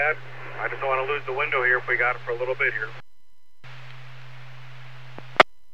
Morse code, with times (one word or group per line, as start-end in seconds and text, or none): none